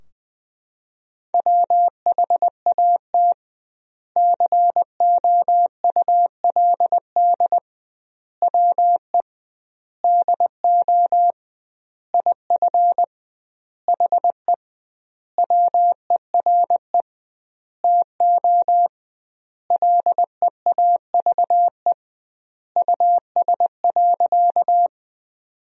Morse code, heard as a keyed tone, about 21 wpm, 700 Hz. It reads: WHAT COULD WE DO IF HE WERE TO LEAVE US.
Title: 2x Word-Spacing - Sentences from Top 400 Words - 20wpm